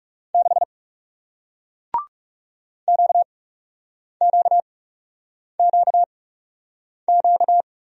Morse code, heard as a keyed tone, about 41 wpm, spaced as for 8 wpm, 700 Hz. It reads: 6 QQQQ